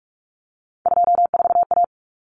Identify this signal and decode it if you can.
Morse code, heard as a keyed tone, about 44 wpm, 700 Hz. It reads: 24A
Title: Common Cut Numbers - 3-Digit 45wpm